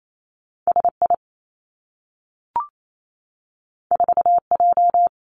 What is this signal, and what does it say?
Morse code, heard as a keyed tone, about 28 wpm, 700 Hz. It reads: S I 4 J
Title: Two Challenging Letter-Number Combinations 28wpm